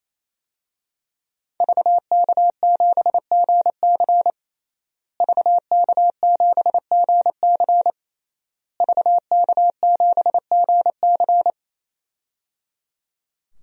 Morse code, heard as a keyed tone, about 28 wpm, 700 Hz. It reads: VK7GC VK7GC VK7GC
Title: DXCC-100 Call Signs - 5 Characters - Repeated 3x - 28wpm